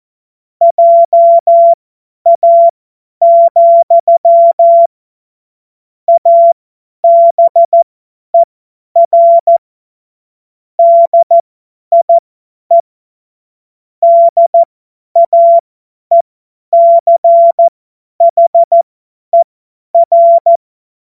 Morse code, heard as a keyed tone, about 14 wpm, spaced as for 10 wpm, 680 Hz. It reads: JA, ABER DIE DAECHER
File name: Morsekurs / Witze / Witz 01. Juni 2021, 50 F70